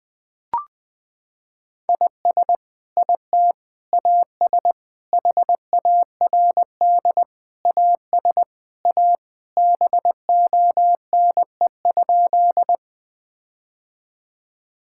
Morse code, heard as a keyed tone, about 21 wpm, 700 Hz. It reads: IS IT AS HARD AS A BONE?
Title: Sentences from Top 1600 Words 20wpm